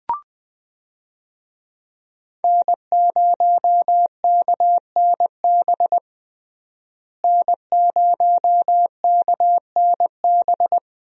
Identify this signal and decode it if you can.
Morse code, heard as a keyed tone, about 21 wpm, 700 Hz. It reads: N0KNB N0KNB
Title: USA Call Signs - 5 Characters - Repeated 2x - 20wpm